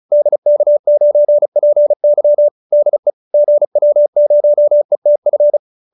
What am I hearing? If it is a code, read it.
Morse code, heard as a keyed tone, about 34 wpm, 570 Hz. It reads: DK9PY DE GW0ETF